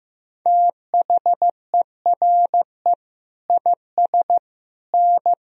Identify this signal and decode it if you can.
Morse code, heard as a keyed tone, about 15 wpm, 700 Hz. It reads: THERE IS N
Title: Sentences from Top 200 Words and Abbreviations 15wpm